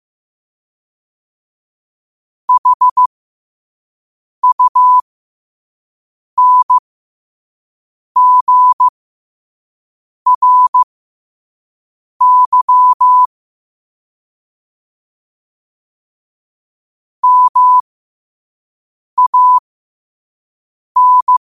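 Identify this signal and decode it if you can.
Morse code, heard as a keyed tone, about 15 wpm, spaced as for 5 wpm, 1000 Hz. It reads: HUNGRY MAN
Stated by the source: Morse code quote of the day sent at 5 wpm.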